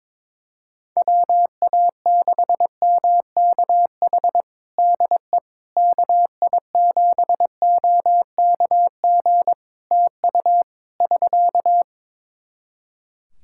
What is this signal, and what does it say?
Morse code, heard as a keyed tone, about 22 wpm, 700 Hz. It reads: WA6MKH DE KI7OKG TU <SK>
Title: Practice QSO - Exchange 4 - Ending - 22wpm